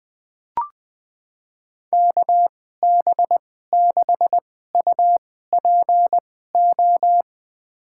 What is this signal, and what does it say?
Morse code, heard as a keyed tone, about 21 wpm, spaced as for 15 wpm, 700 Hz. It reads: KB6UPO
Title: USA Call Signs - 6 Characters - 2x Character Spacing - 20wpm